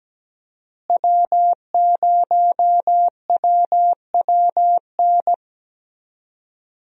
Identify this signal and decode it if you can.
Morse code, heard as a keyed tone, about 17 wpm, 700 Hz. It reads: W0WWN
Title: USA Call Signs - 5 Characters - Repeated 3x - 17wpm